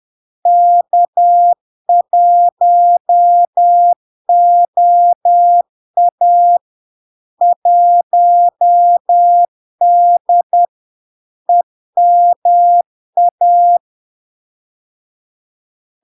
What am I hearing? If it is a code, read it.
Morse code, heard as a keyed tone, about 10 wpm, 700 Hz. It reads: K1OA 1D EMA
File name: LARK-FD-2022-Log-in-CW-10wpm-700-Hz